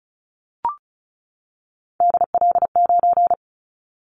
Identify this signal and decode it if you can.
Morse code, heard as a keyed tone, about 34 wpm, 700 Hz. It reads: DL9
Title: Callsign Prefixes for Top 25 DXCC - 35wpm